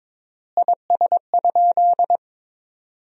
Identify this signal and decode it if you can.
Morse code, heard as a keyed tone, about 22 wpm, 700 Hz. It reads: IS?
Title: 3x Word-Spacing - Sentences from Top 100 Words - 22wpm